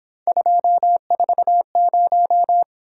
Morse code, heard as a keyed tone, about 26 wpm, 700 Hz. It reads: 240
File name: Speed-Racing - Three Letter-Number Combinations - 22wpm